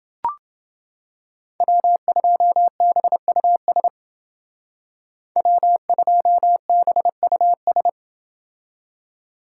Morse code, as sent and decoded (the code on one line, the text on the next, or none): .-- ..--- -... ..- ... / .-- ..--- -... ..- ...
W2BUS W2BUS